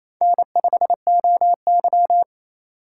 Morse code, heard as a keyed tone, about 28 wpm, 700 Hz. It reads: N5OY
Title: USA Call Signs - 2x Character Spacing - 28wpm